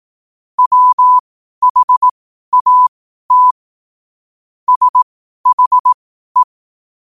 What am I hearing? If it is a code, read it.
Morse code, heard as a keyed tone, about 18 wpm, spaced as for 12 wpm, 1000 Hz. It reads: WHAT SHE
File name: qotdfiles / 12wpm